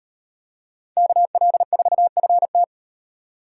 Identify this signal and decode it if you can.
Morse code, heard as a keyed tone, about 36 wpm, 700 Hz. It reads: KL4FT